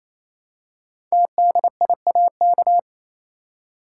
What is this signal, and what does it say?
Morse code, heard as a keyed tone, about 29 wpm, 700 Hz. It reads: TDIAK
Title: Code Groups - 5 Letters - 28wpm